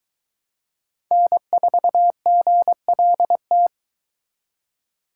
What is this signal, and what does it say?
Morse code, heard as a keyed tone, about 22 wpm, 700 Hz. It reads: N4GLT